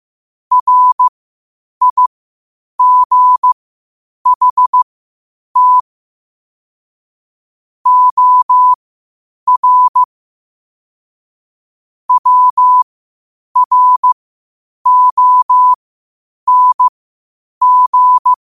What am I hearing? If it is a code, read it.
Morse code, heard as a keyed tone, about 15 wpm, spaced as for 8 wpm, 1000 Hz. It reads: RIGHT OR WRONG
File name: qotdfiles / 8wpm